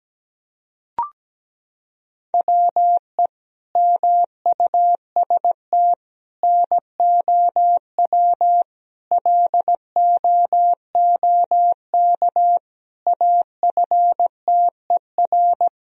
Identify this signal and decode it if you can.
Morse code, heard as a keyed tone, about 17 wpm, 700 Hz. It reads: WE MUST NOW LOOK AFTER
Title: Sentences from Top 1000 Words - Review Part 4 - 17wpm